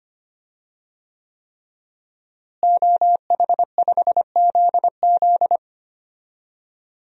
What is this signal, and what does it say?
Morse code, heard as a keyed tone, about 25 wpm, 700 Hz. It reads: OH5ZZ
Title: DXCC-100 Call Signs - 5 Characters - Repeated 3x - 25wpm